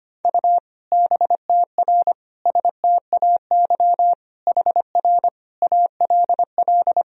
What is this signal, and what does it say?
Morse code, heard as a keyed tone, about 25 wpm, 700 Hz. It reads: U BTR STAY HR ALL
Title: Sentences from Top 500 Words and Abbreviations 25wpm